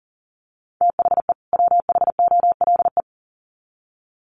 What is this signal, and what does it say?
Morse code, heard as a keyed tone, about 40 wpm, 700 Hz. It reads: THE WHOLE
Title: Sets of 2 Words 40wpm